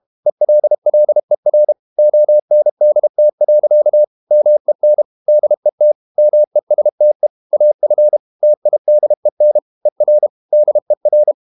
Morse code, heard as a keyed tone, about 32 wpm, 600 Hz. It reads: ELLER ONDT. MEN DET MESTE AF TIDEN ER DER
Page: Ferie 32 wpm